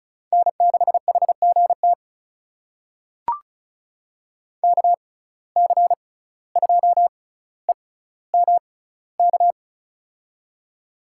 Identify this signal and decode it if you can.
Morse code, heard as a keyed tone, about 36 wpm, spaced as for 17 wpm, 700 Hz. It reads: N6HGT KC2EMK
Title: USA Call Signs - 6x Character Spacing - 35wpm